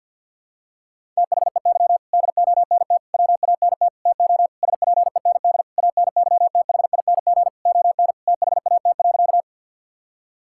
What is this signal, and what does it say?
Morse code, heard as a keyed tone, about 52 wpm, 700 Hz. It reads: THEY DONT WANT TO SPEND ANYTHING ON THAT.
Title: Sentences from Top 1000 Words - Review Part 5 - 50wpm